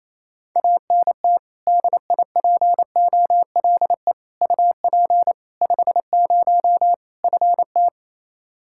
Morse code, heard as a keyed tone, about 27 wpm, 700 Hz. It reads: ANT DIPOLE UP 50 FT
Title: K1USN SST QSO Practice 28wpm